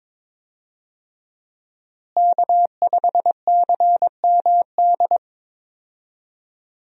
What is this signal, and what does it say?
Morse code, heard as a keyed tone, about 22 wpm, 700 Hz. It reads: K5CMD
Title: USA Call Signs - 5 Characters - Repeated 2x - 22wpm